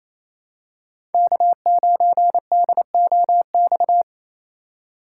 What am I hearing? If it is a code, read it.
Morse code, heard as a keyed tone, about 28 wpm, 700 Hz. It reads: K9DOX
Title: USA Call Signs - 2x Character Spacing - 28wpm